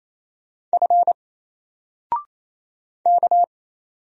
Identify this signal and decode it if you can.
Morse code, heard as a keyed tone, about 27 wpm, 700 Hz. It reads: F K
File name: Single-Letter-Number-28wpm